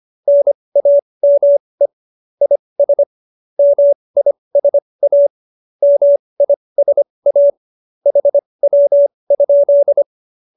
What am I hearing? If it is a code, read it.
Morse code, heard as a keyed tone, about 25 wpm, spaced as for 20 wpm, 570 Hz. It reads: NAME IS MISA MISA HW?
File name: qso216 20